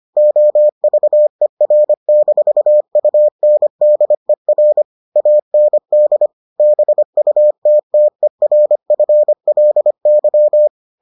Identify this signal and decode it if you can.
Morse code, heard as a keyed tone, about 25 wpm, 600 Hz. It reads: OVER-UNDER AND BUTTERFLY